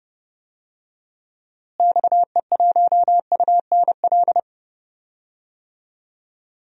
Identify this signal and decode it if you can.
Morse code, heard as a keyed tone, about 30 wpm, 700 Hz. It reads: XE1UNL
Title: DXCC-100 Call Signs - 6 Characters - Repeated 3x - 30wpm